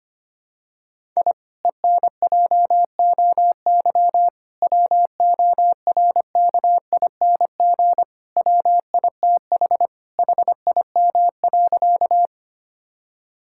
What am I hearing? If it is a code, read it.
Morse code, heard as a keyed tone, about 25 wpm, 700 Hz. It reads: I ENJOY WORKING WITH HIM.
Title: Sentences from Top 1000 Words - Review Part 5 - 25wpm